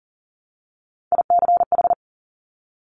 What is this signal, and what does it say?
Morse code, heard as a keyed tone, about 40 wpm, 700 Hz. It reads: ICH
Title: Top Three Letter Combinations in Words 40wpm